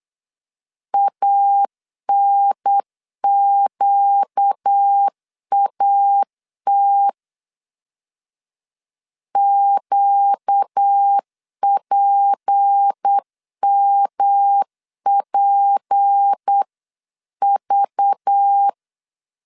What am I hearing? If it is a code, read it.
Morse code, heard as a keyed tone, about 8 wpm, 800 Hz. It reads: ANQAT QPMPV
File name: LEZIONI / 6 PKX0 / 6 P 8wpm 3025